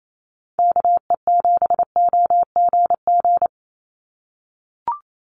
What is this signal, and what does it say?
Morse code, heard as a keyed tone, about 28 wpm, 700 Hz. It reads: KE7OGG
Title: USA Call Signs 28wpm